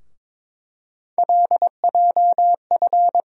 Morse code, heard as a keyed tone, about 21 wpm, 700 Hz. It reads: LJF
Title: USA Call Sign Postfixes - 3 Letters 22wpm